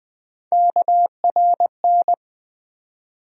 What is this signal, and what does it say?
Morse code, heard as a keyed tone, about 20 wpm, 700 Hz. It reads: KRN
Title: USA Call Sign Postfixes - 3 Letters 20wpm